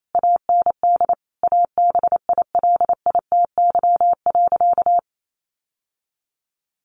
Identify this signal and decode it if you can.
Morse code, heard as a keyed tone, about 28 wpm, 700 Hz. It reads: AND ABILITY.
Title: Sentences from Top 1200 Words 28wpm